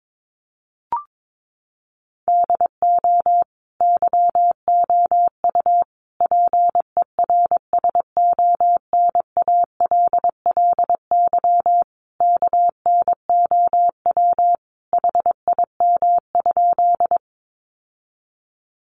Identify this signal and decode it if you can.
Morse code, heard as a keyed tone, about 22 wpm, 700 Hz. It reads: DO YOU PERSONALLY KNOW HIM?
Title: Sentences from Top 1400 Words 22wpm